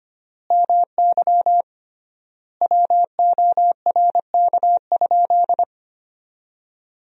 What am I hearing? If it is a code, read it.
Morse code, heard as a keyed tone, about 25 wpm, 700 Hz. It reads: MY WORK?
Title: 3x Word-Spacing - Sentences from Top 500 Words - 25wpm